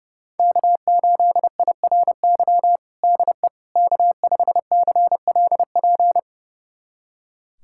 Morse code, heard as a keyed tone, about 30 wpm, 700 Hz. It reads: K8IRY DE K5CLP
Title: USA Callsign DE Callsign - 5 Characters - 30wpm